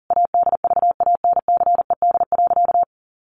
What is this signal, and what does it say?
Morse code, heard as a keyed tone, about 40 wpm, 700 Hz. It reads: ADVANCED.
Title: Sentences from Top 1200 Words 40wpm